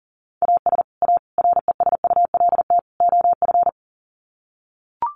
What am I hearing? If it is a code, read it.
Morse code, heard as a keyed tone, about 40 wpm, 700 Hz. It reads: AS A RESULT OF
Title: Sets of 4 Words 40wpm